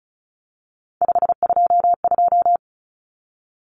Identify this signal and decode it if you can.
Morse code, heard as a keyed tone, about 34 wpm, 700 Hz. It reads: 522
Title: Common Cut Numbers - 3-Digit 35wpm